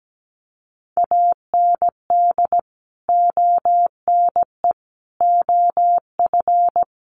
Morse code, heard as a keyed tone, about 17 wpm, 700 Hz. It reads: AND ONE OF
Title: Sets of 5 Words Encore 17wpm